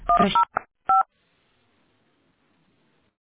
DTMF, heard as telephone touch tones, tones of about 150 ms. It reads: 105